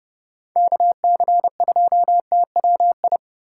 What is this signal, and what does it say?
Morse code, heard as a keyed tone, about 29 wpm, 700 Hz. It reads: KC2TWI